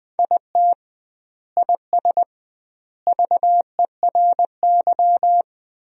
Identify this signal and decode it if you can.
Morse code, heard as a keyed tone, about 20 wpm, 700 Hz. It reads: IT IS VERY
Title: 2x Word-Spacing - Sentences from Top 500 Words - 20wpm